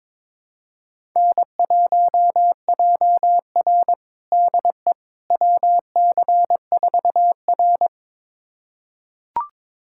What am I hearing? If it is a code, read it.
Morse code, heard as a keyed tone, about 22 wpm, 700 Hz. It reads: N1JR DE WC4R
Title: USA Callsign DE Callsign - 4 Characters - 22wpm